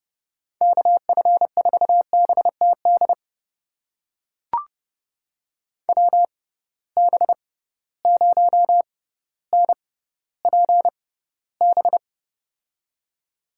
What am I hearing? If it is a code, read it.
Morse code, heard as a keyed tone, about 30 wpm, spaced as for 15 wpm, 700 Hz. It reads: KF4BTD WB0NPB
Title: USA Call Signs - 6 Characters - 6x Character Spacing - 30wpm